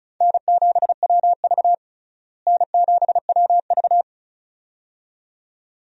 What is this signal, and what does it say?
Morse code, heard as a keyed tone, about 35 wpm, 700 Hz. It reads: N7WV N7WV
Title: USA Call Signs - 4 Characters - Repeated 2x - 35wpm